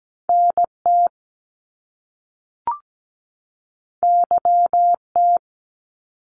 Morse code, heard as a keyed tone, about 17 wpm, 700 Hz. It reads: NT YT
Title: Canadian Provinces and Territories 17wpm